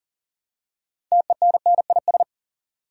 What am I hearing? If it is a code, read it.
Morse code, heard as a keyed tone, about 40 wpm, 700 Hz. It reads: TENNIS
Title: Six Letter Words - 40wpm